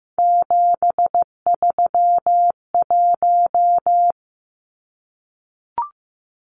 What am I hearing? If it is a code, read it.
Morse code, heard as a keyed tone, about 15 wpm, 700 Hz. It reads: 731
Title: Numbers - 3-Digit 15wpm